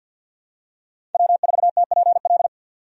Morse code, heard as a keyed tone, about 51 wpm, 700 Hz. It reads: W4TPL
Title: USA Call Signs - 5 Characters - 50wpm